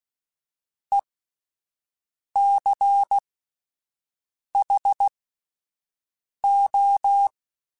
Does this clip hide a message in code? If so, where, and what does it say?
Yes: Morse code from 0.9 to 7.3 s, heard as a keyed tone, about 16 wpm, spaced as for 5 wpm, 800 Hz. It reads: ECHO